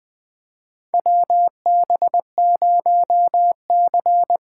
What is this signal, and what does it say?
Morse code, heard as a keyed tone, about 21 wpm, 700 Hz. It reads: WB0C
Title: USA Call Signs - 4 Characters - 4x Character Spacing - 20wpm